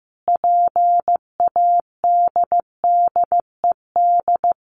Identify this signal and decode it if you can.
Morse code, heard as a keyed tone, about 15 wpm, 700 Hz. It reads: PADDED
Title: A Sound Of Thunder 15wpm